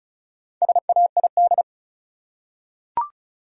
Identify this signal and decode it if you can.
Morse code, heard as a keyed tone, about 34 wpm, 700 Hz. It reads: SAID
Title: Top 100 Words Repeated 35wpm